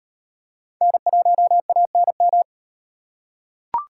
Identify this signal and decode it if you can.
Morse code, heard as a keyed tone, about 38 wpm, 700 Hz. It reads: N1ANM